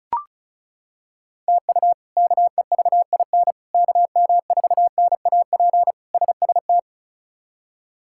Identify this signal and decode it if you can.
Morse code, heard as a keyed tone, about 36 wpm, 700 Hz. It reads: TU KEVIN KM4NAP SST
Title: K1USN SST QSO Practice - 35wpm